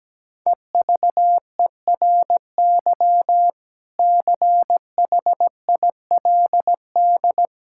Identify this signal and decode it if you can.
Morse code, heard as a keyed tone, about 17 wpm, 700 Hz. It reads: EVERY CHILD
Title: Sentences from Top 1000 Words - Review Part 6 - 17wpm